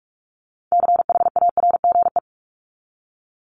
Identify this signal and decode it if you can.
Morse code, heard as a keyed tone, about 44 wpm, 700 Hz. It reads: CHARGE
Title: Top 751-1000 Words 45wpm